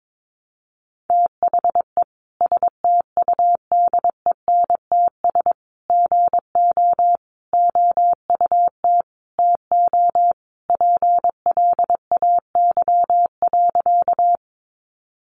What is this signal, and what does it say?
Morse code, heard as a keyed tone, about 22 wpm, 700 Hz. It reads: THE STUDENTS GO OUT TO PLAY.
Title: Easy Phrases 22wpm